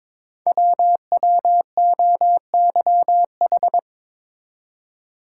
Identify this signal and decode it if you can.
Morse code, heard as a keyed tone, about 22 wpm, 700 Hz. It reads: WWOYH
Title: Code Groups - 5 Letters - 22wpm